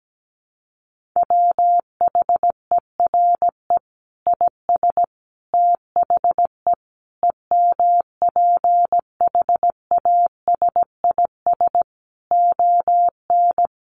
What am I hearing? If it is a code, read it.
Morse code, heard as a keyed tone, about 17 wpm, 700 Hz. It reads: WHERE IS THE EMPHASIS ON
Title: Sentences from Top 1400 Words 17wpm